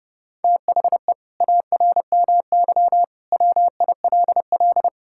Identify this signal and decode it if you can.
Morse code, heard as a keyed tone, about 32 wpm, 700 Hz. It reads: THE ARMY WILL
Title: Sentences from Top 1000 Words - Review Part 5 - 30wpm